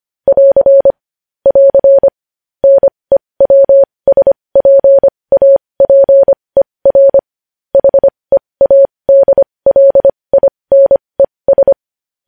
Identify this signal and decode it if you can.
Morse code, heard as a keyed tone, about 25 wpm, spaced as for 20 wpm, 550 Hz. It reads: <AR> <AR> NEWSPAPER HEADLINES
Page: news headlines in morse code at 20 wpm